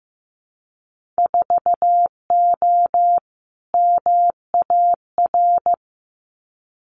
Morse code, heard as a keyed tone, about 15 wpm, 700 Hz. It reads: 4O MAR
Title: Winter Field Day Exchanges Encore 15wpm